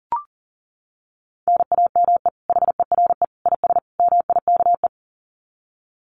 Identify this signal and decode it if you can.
Morse code, heard as a keyed tone, about 40 wpm, 700 Hz. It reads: NAME HERE IS MIKE
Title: Common QSO Phrases 40wpm